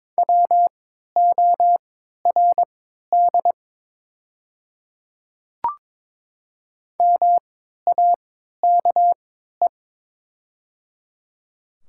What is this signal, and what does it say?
Morse code, heard as a keyed tone, about 22 wpm, spaced as for 13 wpm, 700 Hz. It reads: WORD MAKE
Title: 3x Character Spacing - Top 200 Words - 22wpm